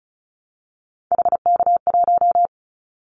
Morse code, heard as a keyed tone, about 34 wpm, 700 Hz. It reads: HK1
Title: Callsign Prefixes for Top 100 DXCC - 35wpm